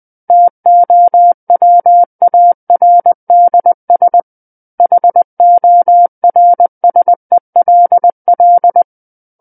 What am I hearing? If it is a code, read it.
Morse code, heard as a keyed tone, about 20 wpm, 700 Hz. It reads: TOWARDS HORSELL
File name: 20 WPM